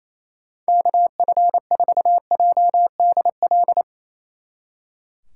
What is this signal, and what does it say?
Morse code, heard as a keyed tone, about 35 wpm, 700 Hz. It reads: KF4JDL